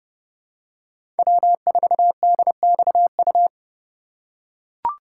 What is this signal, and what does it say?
Morse code, heard as a keyed tone, about 30 wpm, 700 Hz. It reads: W4DXU
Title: USA Call Signs - 5 Characters - 2x Character Spacing - 30wpm